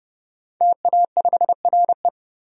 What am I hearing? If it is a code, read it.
Morse code, heard as a keyed tone, about 30 wpm, 700 Hz. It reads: TA5RE
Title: DXCC-100 Call Signs - 5 Characters - 30wpm